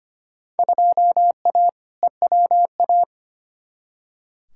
Morse code, heard as a keyed tone, about 25 wpm, 700 Hz. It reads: 2A EWA
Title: Field Day Exchanges - 25wpm